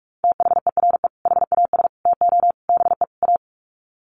Morse code, heard as a keyed tone, about 44 wpm, 700 Hz. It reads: THERE HAS TO BE A
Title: Sets of 5 Words Encore 45wpm